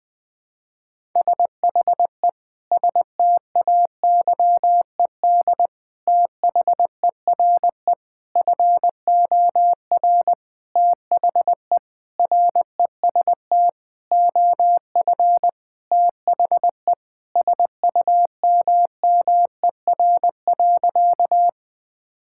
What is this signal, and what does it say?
Morse code, heard as a keyed tone, about 20 wpm, 700 Hz. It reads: SHE STAYED THERE FOR THE REST OF THE SUMMER.
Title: Sentences with 5-Word Phrases - Spoken Beginning - 20wpm